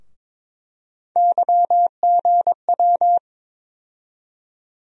Morse code, heard as a keyed tone, about 22 wpm, 700 Hz. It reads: YGW
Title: USA Call Sign Postfixes - 3 Letters 22wpm